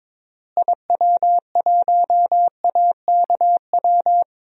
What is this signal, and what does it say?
Morse code, heard as a keyed tone, about 22 wpm, 700 Hz. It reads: IW1AKW
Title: DXCC-100 Call Signs - 6 Characters - Repeated 3x - 22wpm